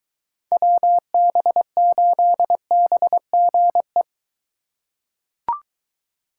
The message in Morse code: WB8BGE

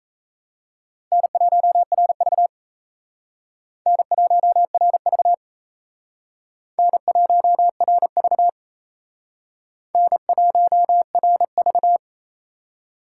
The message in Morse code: N1RV N1RV N1RV N1RV

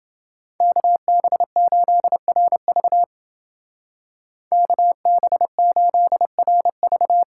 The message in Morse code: KB8RV KB8RV

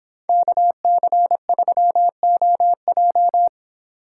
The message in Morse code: KC3OJ

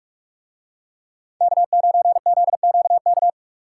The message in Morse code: K9ZQK

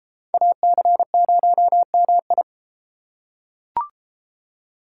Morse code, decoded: AC0MI

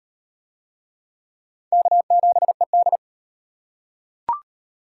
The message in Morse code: K7ED

